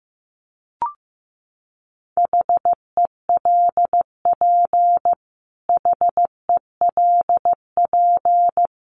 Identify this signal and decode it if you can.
Morse code, heard as a keyed tone, about 15 wpm, 700 Hz. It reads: HELP HELP
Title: Top 500 Words Repeated 15wpm